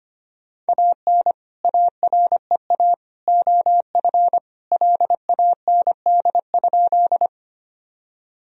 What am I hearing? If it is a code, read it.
Morse code, heard as a keyed tone, about 25 wpm, 700 Hz. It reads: AN AREA OF LAND?
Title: Sentences from Top 1000 Words - Review Part 2 - 25wpm